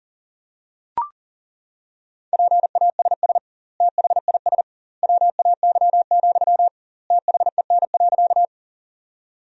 Morse code, heard as a keyed tone, about 37 wpm, 700 Hz. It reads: PASS THIS WAY, THEN.
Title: 2x Word-Spacing - Sentences from Top 400 Words - 40wpm